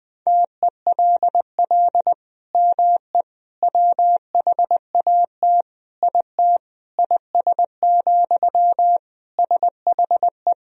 Morse code, heard as a keyed tone, about 20 wpm, 700 Hz. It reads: TELL ME WHAT IT IS, SHE
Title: Sentences from Top 1000 Words - Review Part 1 - 20wpm